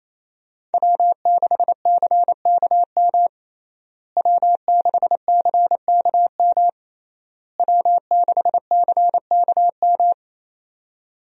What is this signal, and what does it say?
Morse code, heard as a keyed tone, about 27 wpm, 700 Hz. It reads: W6CKM W6CKM W6CKM
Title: USA Call Signs - 5 Characters - Repeated 3x - 28wpm